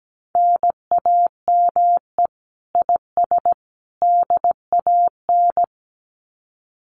Morse code, heard as a keyed tone, about 17 wpm, 700 Hz. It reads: NAME IS DAN
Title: Common QSO Phrases 17wpm